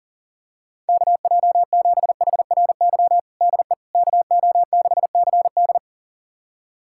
Morse code, heard as a keyed tone, about 40 wpm, 700 Hz. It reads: KJ7HRY DE KO6CD
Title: USA Callsign DE Callsign 40wpm